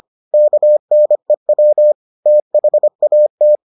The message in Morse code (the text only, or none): KNEW THAT